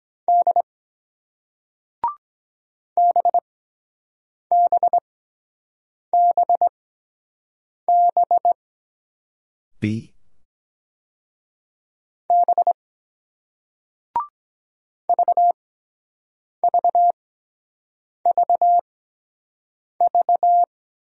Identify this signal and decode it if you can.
Morse code, heard as a keyed tone, about 23 wpm, spaced as for 7 wpm, 700 Hz. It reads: D BBBB B VVVV